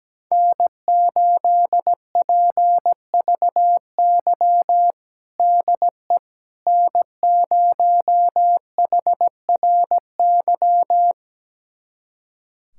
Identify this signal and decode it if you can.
Morse code, heard as a keyed tone, about 17 wpm, 700 Hz. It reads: N8PVY DE N0HRY
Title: USA Callsign DE Callsign - 5 Characters - 17wpm